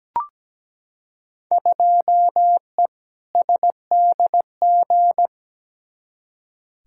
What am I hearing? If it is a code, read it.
Morse code, heard as a keyed tone, about 17 wpm, 700 Hz. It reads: 2E SDG
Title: Field Day Exchanges - 17wpm